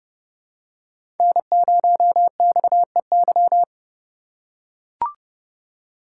Morse code, heard as a keyed tone, about 30 wpm, 700 Hz. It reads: N0XEY
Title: USA Call Signs - 4x Character Spacing - 30wpm